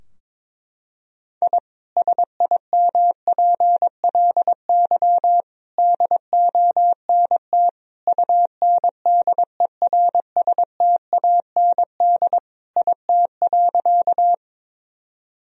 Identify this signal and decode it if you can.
Morse code, heard as a keyed tone, about 22 wpm, 700 Hz. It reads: I SIMPLY DONT UNDERSTAND IT.
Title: Sentences from Top 1000 Words - Review Part 4 - 22wpm